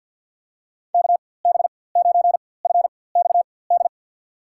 Morse code, heard as a keyed tone, about 52 wpm, spaced as for 23 wpm, 700 Hz. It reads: KB9FXD
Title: USA Call Signs - 6 Characters - 4x Character Spacing - 50wpm